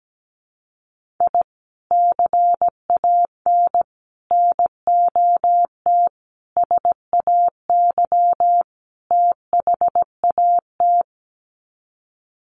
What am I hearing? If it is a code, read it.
Morse code, heard as a keyed tone, about 17 wpm, 700 Hz. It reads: I CAN NOT SAY THAT
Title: Sets of 5 Words Encore2 17wpm